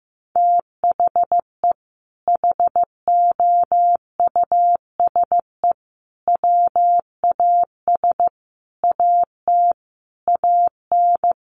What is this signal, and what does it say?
Morse code, heard as a keyed tone, about 15 wpm, 700 Hz. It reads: THE HOUSE WAS AT AN
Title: Sentences from Top 1500 Words 15wpm